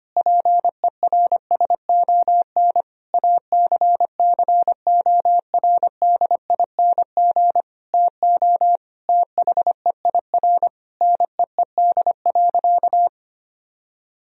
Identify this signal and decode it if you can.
Morse code, heard as a keyed tone, about 25 wpm, 700 Hz. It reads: PERSON ACCORDING TO THEIR NEED.
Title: Sentences from Top 1000 Words - Review Part 4 - 25wpm